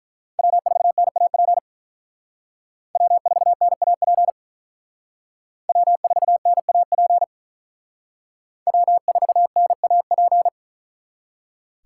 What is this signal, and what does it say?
Morse code, heard as a keyed tone, about 44 wpm, 700 Hz. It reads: W4NAP W4NAP W4NAP W4NAP